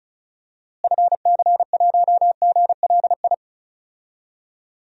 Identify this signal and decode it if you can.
Morse code, heard as a keyed tone, about 36 wpm, 700 Hz. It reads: FC1GLI
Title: DXCC-100 Call Signs - 6 Characters - Repeated 2x - 35wpm